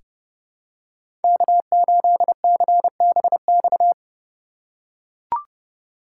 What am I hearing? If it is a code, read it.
Morse code, heard as a keyed tone, about 29 wpm, 700 Hz. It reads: K8CBX
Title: USA Call Signs - 8x Character Spacing - 30wpm